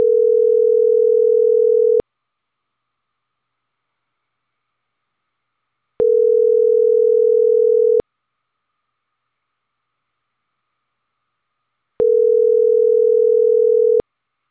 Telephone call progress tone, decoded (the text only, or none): ringback tone